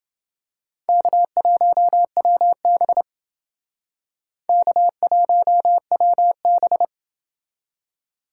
Morse code, heard as a keyed tone, about 29 wpm, 700 Hz. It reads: K1WB K1WB